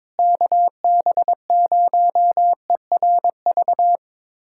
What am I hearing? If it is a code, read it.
Morse code, heard as a keyed tone, about 22 wpm, 700 Hz. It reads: KB0ERV
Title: USA Callsign DE Callsign 22wpm